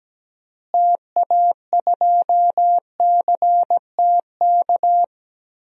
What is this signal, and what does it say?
Morse code, heard as a keyed tone, about 17 wpm, 700 Hz. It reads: TA2CTK
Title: DXCC-100 Call Signs - 6 Characters - Repeated 3x - 17wpm